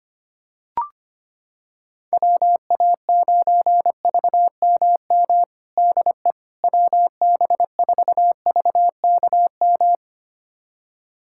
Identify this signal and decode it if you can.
Morse code, heard as a keyed tone, about 25 wpm, 700 Hz. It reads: WA9VMM DE WB4VKM
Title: USA Callsign DE Callsign - 6 Characters - 25wpm